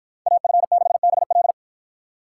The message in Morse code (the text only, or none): AF6BL